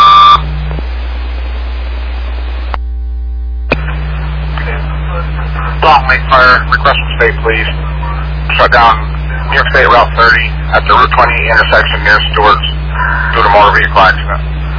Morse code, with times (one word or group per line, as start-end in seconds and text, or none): none